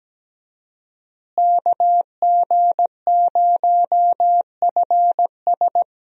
Morse code, heard as a keyed tone, about 17 wpm, 700 Hz. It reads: KG0FS
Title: USA Call Signs - 5 Characters - Repeated 2x - 17wpm